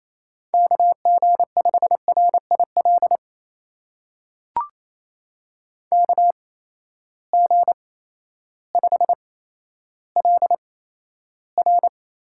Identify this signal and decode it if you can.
Morse code, heard as a keyed tone, about 27 wpm, spaced as for 13 wpm, 700 Hz. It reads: KG5RIL KG5LR
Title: USA Call Signs - 6 Characters - 8x Character Spacing - 28wpm